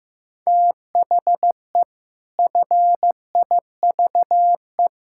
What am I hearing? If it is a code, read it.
Morse code, heard as a keyed tone, about 15 wpm, 700 Hz. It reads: THE FIVE
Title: Pangram Challenge v2 - 15wpm